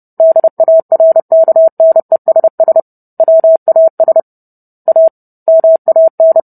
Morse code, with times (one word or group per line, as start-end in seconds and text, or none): 0.2-2.8 DARKNESS
3.2-4.2 WAS
4.9-5.1 A
5.5-6.4 MAN